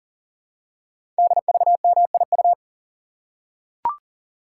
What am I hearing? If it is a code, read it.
Morse code, heard as a keyed tone, about 38 wpm, 690 Hz. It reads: DVMIU
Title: Code Groups - 5 Letters - 40wpm